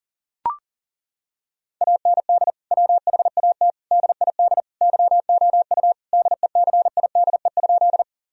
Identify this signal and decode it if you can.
Morse code, heard as a keyed tone, about 40 wpm, 700 Hz. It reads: AND WHAT DID YOU DECIDE?
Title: Sentences from Top 1000 Words - Review Part 3 - 40wpm